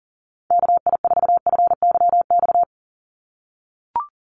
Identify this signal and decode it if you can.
Morse code, heard as a keyed tone, about 40 wpm, 690 Hz. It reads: KI4FYX
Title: USA Call Signs 40wpm